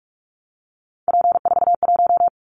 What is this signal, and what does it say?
Morse code, heard as a keyed tone, about 46 wpm, 700 Hz. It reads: P41
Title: Callsign Prefixes for Top 100 DXCC - 45wpm